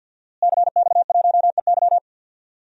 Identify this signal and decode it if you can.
Morse code, heard as a keyed tone, about 49 wpm, 700 Hz. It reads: CX1EY
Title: DXCC-100 Call Signs - 5 Characters - Repeated 3x - 50wpm